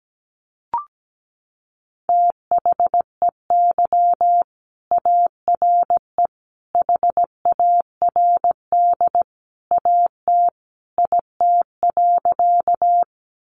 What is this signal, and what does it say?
Morse code, heard as a keyed tone, about 17 wpm, 700 Hz. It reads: THEY ARE HARD AT IT.
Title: Sentences from Top 300 Words 17wpm